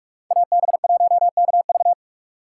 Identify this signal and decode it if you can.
Morse code, heard as a keyed tone, about 44 wpm, 700 Hz. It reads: AB1KV